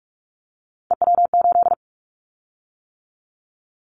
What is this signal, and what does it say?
Morse code, heard as a keyed tone, about 44 wpm, 700 Hz. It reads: EW8
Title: Callsign Prefixes for Top 100 DXCC - 45wpm